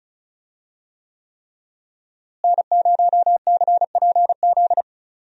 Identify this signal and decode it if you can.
Morse code, heard as a keyed tone, about 36 wpm, 700 Hz. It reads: N0CPZ
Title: USA Call Signs - 5 Characters - Repeated 3x - 35wpm